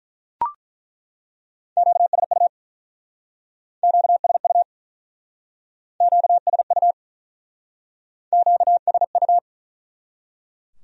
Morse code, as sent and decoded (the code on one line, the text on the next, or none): --.- ... ..- / --.- ... ..- / --.- ... ..- / --.- ... ..-
QSU QSU QSU QSU